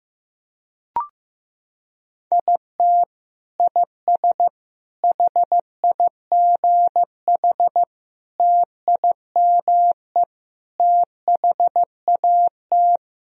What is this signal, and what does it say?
Morse code, heard as a keyed tone, about 15 wpm, 700 Hz. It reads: IT IS HIGH TIME THAT
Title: Sentences from Top 1000 Words - Review Part 1 - 15wpm